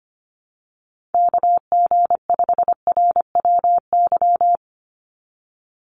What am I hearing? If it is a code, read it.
Morse code, heard as a keyed tone, about 25 wpm, 700 Hz. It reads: KG5RWY
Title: USA Call Signs 25wpm